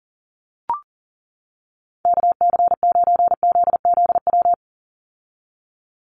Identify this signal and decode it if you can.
Morse code, heard as a keyed tone, about 40 wpm, 700 Hz. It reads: KC9ZZW
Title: USA Call Signs 40wpm